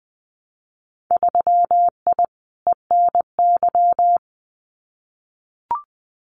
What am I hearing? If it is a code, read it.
Morse code, heard as a keyed tone, about 20 wpm, 700 Hz. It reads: 3I ENY
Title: Winter Field Day Exchanges Encore 20wpm